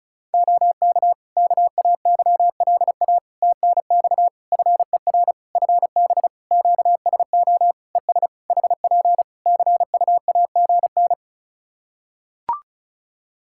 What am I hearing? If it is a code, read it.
Morse code, heard as a keyed tone, about 36 wpm, 700 Hz. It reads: OK KAYLA TNX FER FB QSO ES HP CUAGN
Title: Practice QSO - Exchange 4 - Ending - 35wpm